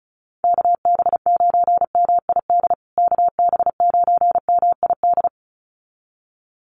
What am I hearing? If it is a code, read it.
Morse code, heard as a keyed tone, about 34 wpm, 700 Hz. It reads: KB9MID KB9MID
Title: USA Call Signs Repeated 2x 35wpm